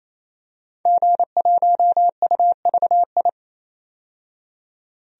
Morse code, heard as a keyed tone, about 28 wpm, 700 Hz. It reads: G1UVI
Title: DXCC-100 Call Signs - 5 Characters - 28wpm